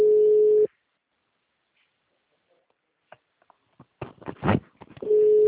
Ringback tone (Europe, 425 Hz).